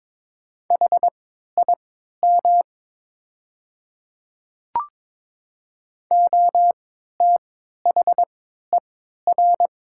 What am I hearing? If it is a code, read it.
Morse code, heard as a keyed tone, about 22 wpm, spaced as for 13 wpm, 700 Hz. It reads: HIM OTHER
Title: 3x Character Spacing - Top 200 Words - 22wpm